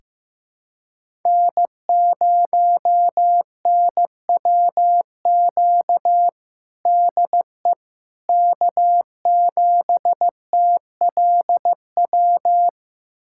Morse code, heard as a keyed tone, about 15 wpm, 700 Hz. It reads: N0NWQ DE K7TLW
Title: USA Callsign DE Callsign - 5 Characters - 15wpm